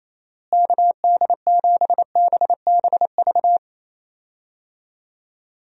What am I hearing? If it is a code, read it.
Morse code, heard as a keyed tone, about 28 wpm, 700 Hz. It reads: KD7BBV
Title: USA Call Signs - 6 Characters - Repeated 2x - 28wpm